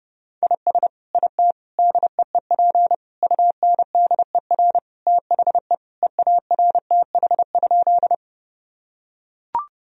Morse code, heard as a keyed tone, about 30 wpm, 700 Hz. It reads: IS IT DEEP UNDER THE EARTH?
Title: Sentences from Top 1000 Words - Review Part 3 - 30wpm